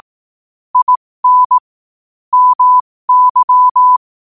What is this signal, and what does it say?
Morse code, heard as a keyed tone, about 18 wpm, spaced as for 15 wpm, 1000 Hz. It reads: IN MY
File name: qotdfiles / 15wpm